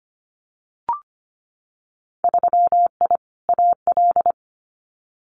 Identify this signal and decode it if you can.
Morse code, heard as a keyed tone, about 25 wpm, 700 Hz. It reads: 3I AL
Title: Winter Field Day Exchanges Encore 25wpm